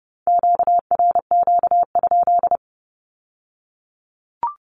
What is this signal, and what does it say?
Morse code, heard as a keyed tone, about 30 wpm, 700 Hz. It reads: QRQ?
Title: Q-Codes with Spoken Meaning - 30wpm